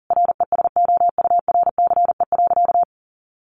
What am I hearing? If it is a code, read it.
Morse code, heard as a keyed tone, about 40 wpm, 700 Hz. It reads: RESOURCE.
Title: Sentences from Top 1300 Words 40wpm